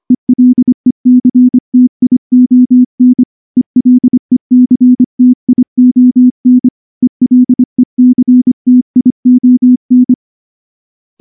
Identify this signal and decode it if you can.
Morse code, heard as a keyed tone, about 24 wpm, 250 Hz. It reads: ELECTION ELECTION ELECTION
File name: morse / morse-training-25WPM